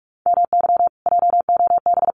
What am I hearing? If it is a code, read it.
Morse code, heard as a keyed tone, about 44 wpm, 700 Hz. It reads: MY JOB
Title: Common QSO Phrases Encore 45wpm